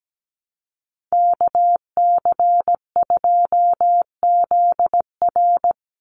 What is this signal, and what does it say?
Morse code, heard as a keyed tone, about 17 wpm, 700 Hz. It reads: KC2ZR
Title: USA Call Signs 17wpm